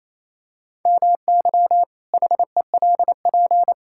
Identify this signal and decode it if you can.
Morse code, heard as a keyed tone, about 28 wpm, 700 Hz. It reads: MY HELP
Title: Pangram Challenge v2 - 28wpm